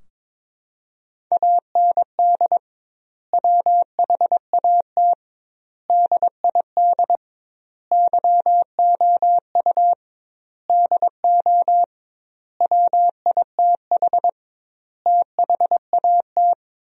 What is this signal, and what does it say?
Morse code, heard as a keyed tone, about 22 wpm, 700 Hz. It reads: AND WHAT DID YOU DO WITH THAT
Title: 2x Word-Spacing - Sentences from Top 300 Words - 22wpm